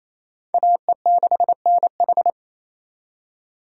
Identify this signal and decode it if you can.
Morse code, heard as a keyed tone, about 36 wpm, 700 Hz. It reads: AE6NH